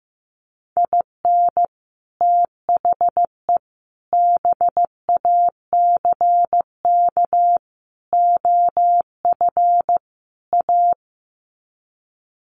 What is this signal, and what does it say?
Morse code, heard as a keyed tone, about 15 wpm, 700 Hz. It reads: IN THE BACK OF A
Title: Sets of 5 Words Encore2 15wpm